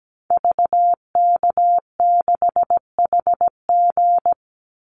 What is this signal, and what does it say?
Morse code, heard as a keyed tone, about 17 wpm, 700 Hz. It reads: VK6HG
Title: Call Signs Repeated 2x - 17wpm